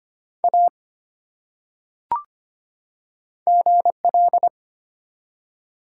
Morse code, heard as a keyed tone, about 25 wpm, 700 Hz. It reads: A GL
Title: Top 180 QSO Elements 25wpm